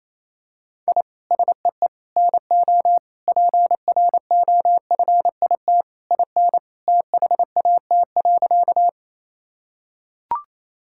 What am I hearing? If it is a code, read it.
Morse code, heard as a keyed tone, about 27 wpm, 700 Hz. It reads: I SEE NO PROFIT IN THAT.
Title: Sentences from Top 1000 Words - Review Part 5 - 28wpm